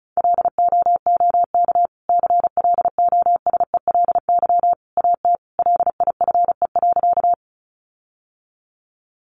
Morse code, heard as a keyed tone, about 34 wpm, 700 Hz. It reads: LOOK CLOSELY AT LIFE.
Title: Sentences from Top 1400 Words 35wpm